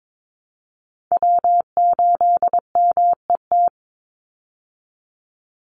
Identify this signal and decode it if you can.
Morse code, heard as a keyed tone, about 22 wpm, 700 Hz. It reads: W8MET
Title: Call Signs Repeated 2x - 22wpm